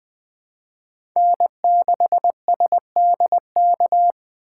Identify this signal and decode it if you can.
Morse code, heard as a keyed tone, about 20 wpm, 700 Hz. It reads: N6SDK
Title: USA Call Signs - 5 Characters - 2x Character Spacing - 20wpm